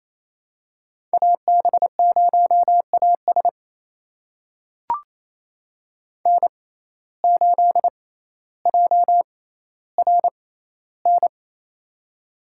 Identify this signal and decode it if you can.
Morse code, heard as a keyed tone, about 28 wpm, spaced as for 14 wpm, 700 Hz. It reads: AB0AS N8JRN
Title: USA Call Signs - 5 Characters - 6x Character Spacing - 28wpm